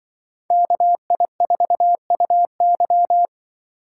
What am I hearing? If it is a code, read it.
Morse code, heard as a keyed tone, about 24 wpm, 700 Hz. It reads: KI4UY
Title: Speed Racing - USA 2x2 Call Signs 20wpm